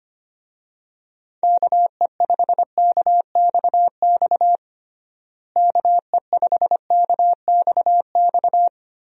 Morse code, heard as a keyed tone, about 25 wpm, 700 Hz. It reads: KE5KXX KE5KXX
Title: USA Call Signs - 6 Characters - Repeated 2x - 25wpm